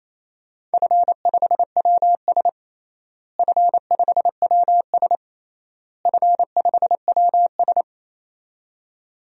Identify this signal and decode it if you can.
Morse code, heard as a keyed tone, about 28 wpm, 700 Hz. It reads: F5WS F5WS F5WS
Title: DXCC-100 Call Signs - 4 Characters - Repeated 3x - 28wpm